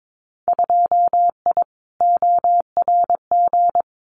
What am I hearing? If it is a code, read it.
Morse code, heard as a keyed tone, about 22 wpm, 700 Hz. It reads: 2I ORG
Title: Winter Field Day Exchanges Encore 22wpm